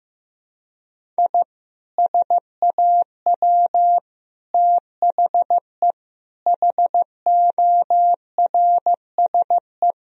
Morse code, heard as a keyed tone, about 15 wpm, 700 Hz. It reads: I SAW THE HORSE
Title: Sentences from Top 1000 Words - Review Part 2 - 15wpm